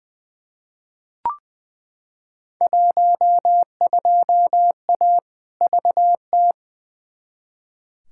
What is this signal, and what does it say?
Morse code, heard as a keyed tone, about 20 wpm, 700 Hz. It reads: 12A VT
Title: Field Day Exchanges Encore - 20wpm